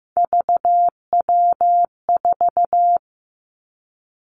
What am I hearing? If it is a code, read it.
Morse code, heard as a keyed tone, about 15 wpm, 700 Hz. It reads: VW4
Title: Three Challenging Letter-Number Combinations 15wpm